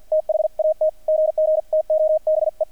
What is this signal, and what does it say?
Morse code, heard as a keyed tone, about 50 wpm, spaced as for 44 wpm, 620 Hz, with radio noise. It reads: THAT GG TO BE
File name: archived / CRSnet